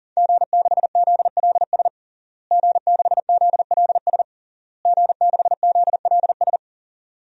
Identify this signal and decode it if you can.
Morse code, heard as a keyed tone, about 37 wpm, 700 Hz. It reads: G6ZLS G6ZLS G6ZLS
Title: DXCC-100 Call Signs - 5 Characters - Repeated 3x - 40wpm